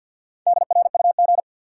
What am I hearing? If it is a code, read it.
Morse code, heard as a keyed tone, about 52 wpm, 700 Hz. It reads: DRUG
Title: Sets of 5 Words - New Word Drill - 50wpm